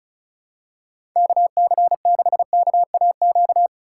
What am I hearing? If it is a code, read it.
Morse code, heard as a keyed tone, about 36 wpm, 700 Hz. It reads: KC6KAQ
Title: USA Callsign DE Callsign - 6 Characters - 35wpm